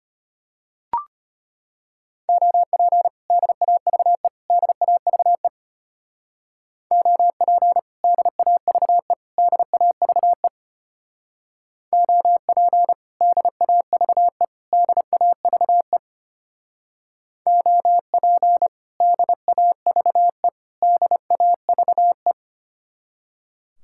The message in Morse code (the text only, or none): OP DAVE DAVE OP DAVE DAVE OP DAVE DAVE OP DAVE DAVE